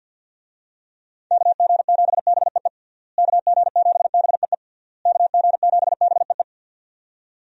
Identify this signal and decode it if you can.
Morse code, heard as a keyed tone, about 52 wpm, 700 Hz. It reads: KG7BEE KG7BEE KG7BEE
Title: USA Call Signs - 6 Characters - Repeated 3x - 50wpm